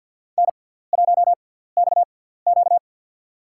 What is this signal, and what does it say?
Morse code, heard as a keyed tone, about 52 wpm, spaced as for 17 wpm, 700 Hz. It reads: N1XQ
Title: USA Call Signs - 6x Character Spacing - 50wpm